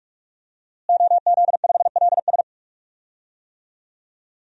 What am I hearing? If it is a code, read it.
Morse code, heard as a keyed tone, about 41 wpm, 700 Hz. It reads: OZ5LS
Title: DXCC-100 Call Signs - 5 Characters - Repeated 3x - 45wpm